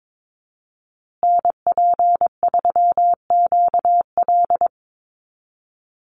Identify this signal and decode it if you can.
Morse code, heard as a keyed tone, about 22 wpm, 700 Hz. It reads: NP3QL
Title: USA Call Signs Repeated 3x 22wpm